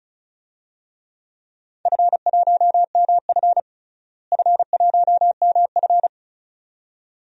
Morse code, heard as a keyed tone, about 36 wpm, 700 Hz. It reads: F1MF F1MF
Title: DXCC-100 Call Signs - 4 Characters - Repeated 2x - 35wpm